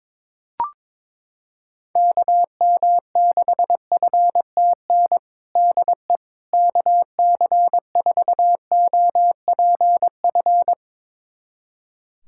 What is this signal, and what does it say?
Morse code, heard as a keyed tone, about 22 wpm, 700 Hz. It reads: KM6FTN DE KC4OPF
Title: USA Callsign DE Callsign 22wpm